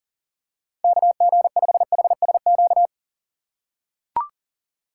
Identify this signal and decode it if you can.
Morse code, heard as a keyed tone, about 40 wpm, 700 Hz. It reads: KG5HSQ
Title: USA Call Signs - 6 Characters - 3x Character Spacing - 40wpm